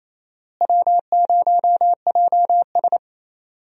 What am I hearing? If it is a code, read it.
Morse code, heard as a keyed tone, about 28 wpm, 700 Hz. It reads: W0JS
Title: USA Call Signs - 4 Characters - Repeated 3x - 28wpm